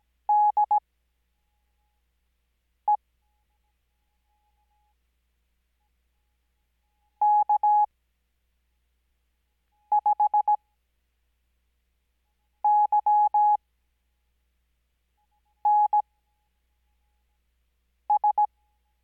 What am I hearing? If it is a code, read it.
Morse code, heard as a keyed tone, about 17 wpm, spaced as for 4 wpm, 840 Hz. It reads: DE K5YNS
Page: Farnsworth Spacing: 15WPM Character speed Spaced at 5WPM.